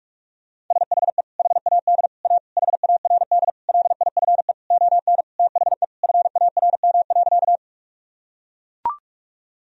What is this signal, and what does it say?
Morse code, heard as a keyed tone, about 41 wpm, 700 Hz. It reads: SHE HAD A HARD LIFE ON THE FARM.
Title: Sentences from Top 1000 Words - Review Part 2 - 45wpm